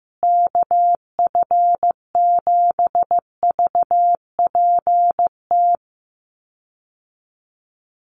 Morse code, heard as a keyed tone, about 15 wpm, 700 Hz. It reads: KF7VPT
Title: USA Call Signs Repeated 3x 15wpm